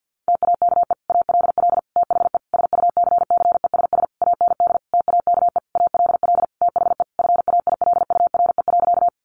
Morse code, heard as a keyed tone, about 50 wpm, 700 Hz. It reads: TAKE ALL THE SUCCESS AND TAKE ALL THE FAILURE.
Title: Sentences from Top 900 Words 50wpm